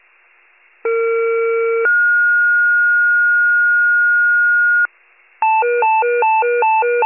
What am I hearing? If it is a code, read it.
Two-tone page, heard as a one-second tone, then a three-second tone, about 460 then 1430 Hz. It is heard from 0.8 to 4.9 s.